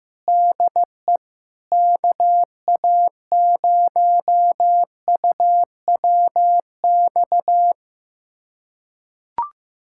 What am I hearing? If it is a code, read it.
Morse code, heard as a keyed tone, about 15 wpm, 700 Hz. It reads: DE KA0UWX
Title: USA Callsign DE Callsign - 6 Characters - 15wpm